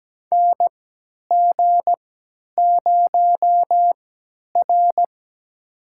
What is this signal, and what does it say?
Morse code, heard as a keyed tone, about 17 wpm, spaced as for 10 wpm, 700 Hz. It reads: NG0R
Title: USA Call Signs - 3x Character Spacing - 17wpm